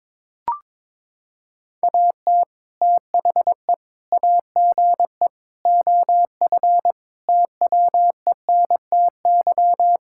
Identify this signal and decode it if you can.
Morse code, heard as a keyed tone, about 21 wpm, 700 Hz. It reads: AT THE AGE OF TWENTY
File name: practice / 22wpm / Sentences with 4-Word Phrases - 22wpm